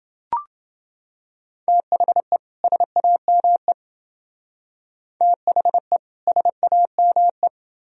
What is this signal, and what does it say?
Morse code, heard as a keyed tone, about 28 wpm, 700 Hz. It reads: THE SAME THE SAME